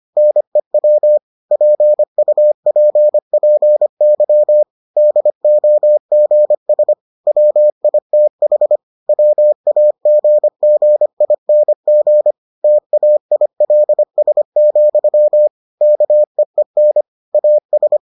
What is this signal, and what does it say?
Morse code, heard as a keyed tone, about 25 wpm, 600 Hz. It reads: NEW PUPPY DOGS WITH WAGGING TAILS, KEEN AS